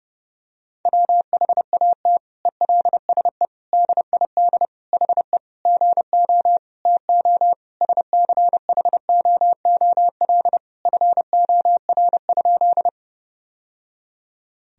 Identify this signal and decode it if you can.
Morse code, heard as a keyed tone, about 32 wpm, 700 Hz. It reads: WHAT ELSE DID HE GO TO SCHOOL FOR?
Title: Sentences from Top 1000 Words - Review Part 1 - 30wpm